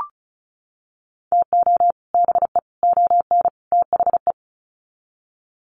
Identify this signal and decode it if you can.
Morse code, heard as a keyed tone, about 34 wpm, 700 Hz. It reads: TO BE ON THE
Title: Sets of 4 Words 35wpm